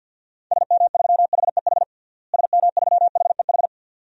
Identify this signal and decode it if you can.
Morse code, heard as a keyed tone, about 48 wpm, 700 Hz. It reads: SM3HEH SM3HEH
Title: DXCC-100 Call Signs - 6 Characters - Repeated 2x - 50wpm